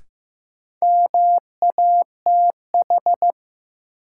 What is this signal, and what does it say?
Morse code, heard as a keyed tone, about 15 wpm, 700 Hz. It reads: MATH
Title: Sentences from Top 1000 Words - Review Part 6 - New Word Drill - 15wpm